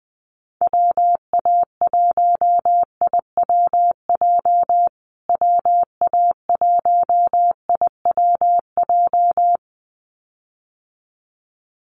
Morse code, heard as a keyed tone, about 20 wpm, 700 Hz. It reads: WA1IWJ WA1IWJ
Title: USA Call Signs Repeated 2x 20wpm